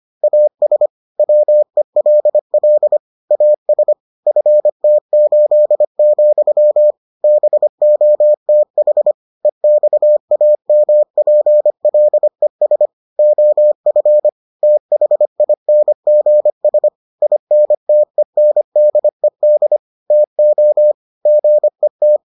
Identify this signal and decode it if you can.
Morse code, heard as a keyed tone, about 25 wpm, 600 Hz. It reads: AS WELL AS FT8, BOTH EXAMPLES OF THINGS INTENDED TO GET